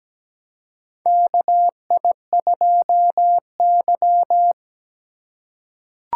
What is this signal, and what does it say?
Morse code, heard as a keyed tone, about 17 wpm, 700 Hz. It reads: KI2Y
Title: USA Call Signs - 4 Characters - 2x Character Spacing - 17wpm